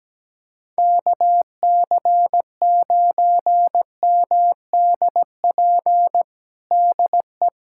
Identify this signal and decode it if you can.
Morse code, heard as a keyed tone, about 17 wpm, 700 Hz. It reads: KC9MDP DE
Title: USA Callsign DE Callsign - 6 Characters - 17wpm